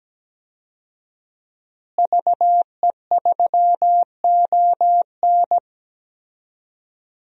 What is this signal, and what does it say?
Morse code, heard as a keyed tone, about 17 wpm, 700 Hz. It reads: VE3ON
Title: DXCC-100 Call Signs - 5 Characters - Repeated 2x - 17wpm